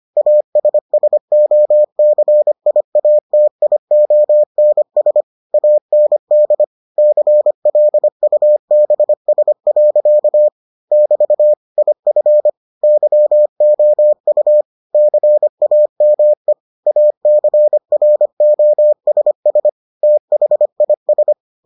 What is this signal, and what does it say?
Morse code, heard as a keyed tone, about 25 wpm, 600 Hz. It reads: ASSOCIATIONS AND CLUBS. <BT> IF YOU CAME ACROSS THIS